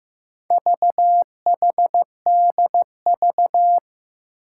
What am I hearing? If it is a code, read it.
Morse code, heard as a keyed tone, about 15 wpm, 700 Hz. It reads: VHDV
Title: Four Challenging Letter-Number Combinations 15wpm